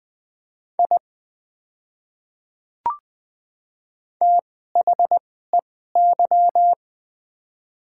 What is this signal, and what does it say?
Morse code, heard as a keyed tone, about 20 wpm, spaced as for 14 wpm, 700 Hz. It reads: I THEY
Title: 2x Character Spacing - Top 200 Words - 20wpm